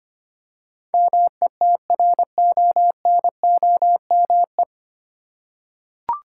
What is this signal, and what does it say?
Morse code, heard as a keyed tone, about 25 wpm, 700 Hz. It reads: METRONOME
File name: practice / 25wpm / Difficult Words with Hints - 25wpm